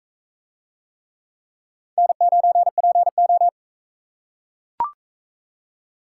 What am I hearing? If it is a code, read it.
Morse code, heard as a keyed tone, about 42 wpm, 700 Hz. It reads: N9PO